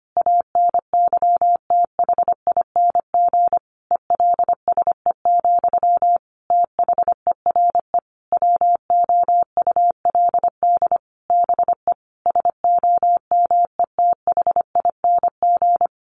Morse code, heard as a keyed tone, about 25 wpm, 700 Hz. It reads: ANYTHING ELSE, THERE WOULD BE SOMETHING
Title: Sentences from Top 1200 Words 25wpm